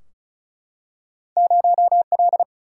Morse code, heard as a keyed tone, about 36 wpm, 700 Hz. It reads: 0L
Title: Single-Letter Single-Number 35wpm